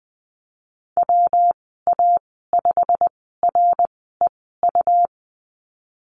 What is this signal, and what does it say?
Morse code, heard as a keyed tone, about 20 wpm, spaced as for 15 wpm, 700 Hz. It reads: WA5REU
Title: 122 Lesson 20g - Number 3 - Call Signs 20wpm